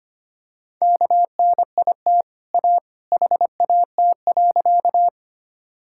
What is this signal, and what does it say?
Morse code, heard as a keyed tone, about 25 wpm, 700 Hz. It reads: KNIT A HAT.
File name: Easy Sentences with Hints - 25wpm